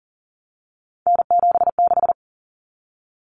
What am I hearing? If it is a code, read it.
Morse code, heard as a keyed tone, about 40 wpm, 700 Hz. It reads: N76
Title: Common Cut Numbers - 3-Digit 40wpm